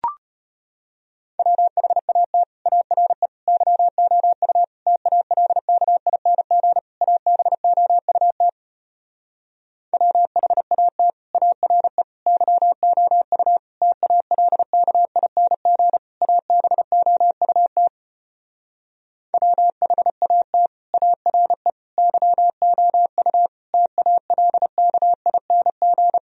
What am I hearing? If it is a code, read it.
Morse code, heard as a keyed tone, about 33 wpm, 700 Hz. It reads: WHAT ARE YOU TALKING ABOUT WHAT ARE YOU TALKING ABOUT WHAT ARE YOU TALKING